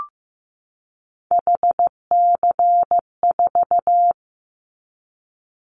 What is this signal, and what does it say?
Morse code, heard as a keyed tone, about 15 wpm, 700 Hz. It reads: HC4
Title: Callsign Prefixes for Top 100 DXCC - 15wpm